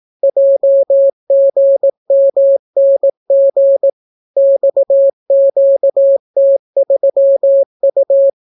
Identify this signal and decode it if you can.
Morse code, heard as a keyed tone, about 18 wpm, 550 Hz. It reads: JGMNG XQT3U